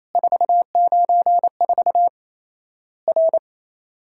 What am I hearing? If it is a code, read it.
Morse code, heard as a keyed tone, about 28 wpm, 700 Hz. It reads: 494 R
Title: Worked All Europe DX Contest - QTC Exchange - 28wpm